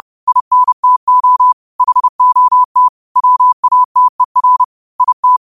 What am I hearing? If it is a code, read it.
Morse code, heard as a keyed tone, about 30 wpm, 1000 Hz. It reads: INTO HOT WATER IT
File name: 30wpm